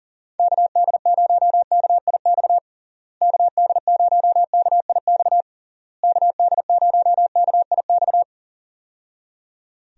Morse code, heard as a keyed tone, about 40 wpm, 700 Hz. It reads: KD0KIX KD0KIX KD0KIX
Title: USA Call Signs - 6 Characters - Repeated 3x - 40wpm